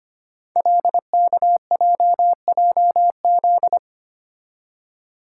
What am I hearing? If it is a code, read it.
Morse code, heard as a keyed tone, about 25 wpm, 690 Hz. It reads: LKJJZ
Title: Code Groups - 5 Letters - 25wpm